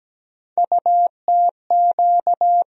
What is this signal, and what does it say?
Morse code, heard as a keyed tone, about 17 wpm, 700 Hz. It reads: UTQ
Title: USA Call Sign Postfixes - 3 Letters 17wpm